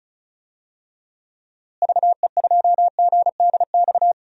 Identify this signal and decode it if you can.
Morse code, heard as a keyed tone, about 36 wpm, 700 Hz. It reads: VE2GDX
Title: DXCC-100 Call Signs - 6 Characters - Repeated 3x - 35wpm